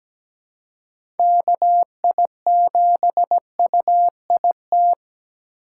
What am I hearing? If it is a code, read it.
Morse code, heard as a keyed tone, about 17 wpm, 700 Hz. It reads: KI7UIT
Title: USA Call Signs - 6 Characters - 4x Character Spacing - 17wpm